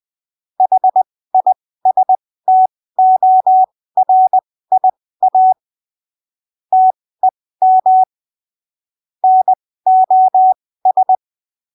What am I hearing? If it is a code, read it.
Morse code, heard as a keyed tone, about 20 wpm, spaced as for 15 wpm, 750 Hz. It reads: HISTORIA TEM NOS
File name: arquivos / texto / texto 05 20ppm